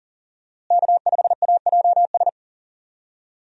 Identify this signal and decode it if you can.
Morse code, heard as a keyed tone, about 40 wpm, 700 Hz. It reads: K5AJS
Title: USA Call Signs - 5 Characters - Repeated 3x - 40wpm